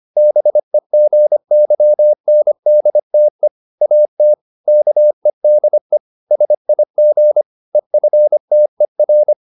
Morse code, heard as a keyed tone, about 25 wpm, 600 Hz. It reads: BEGYNDTE AT KEDE SIG EFTER
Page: Ferie 25 wpm